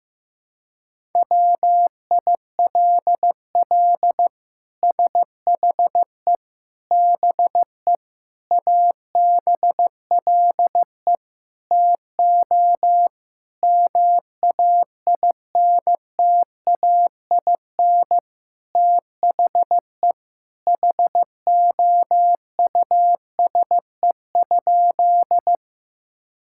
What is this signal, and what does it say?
Morse code, heard as a keyed tone, about 16 wpm, 700 Hz. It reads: WILL SHE BE ABLE TO MAINTAIN THE HOUSE?
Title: Sentences from Top 1000 Words - Review Part 5 - 15wpm